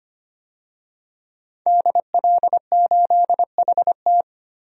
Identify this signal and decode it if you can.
Morse code, heard as a keyed tone, about 25 wpm, 700 Hz. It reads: DL8HT
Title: DXCC-100 Call Signs - 5 Characters - Repeated 3x - 25wpm